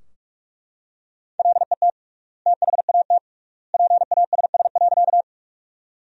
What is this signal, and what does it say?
Morse code, heard as a keyed tone, about 41 wpm, 700 Hz. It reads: LET THAT PASS.
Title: 3x Word-Spacing - Sentences from Top 400 Words - 45wpm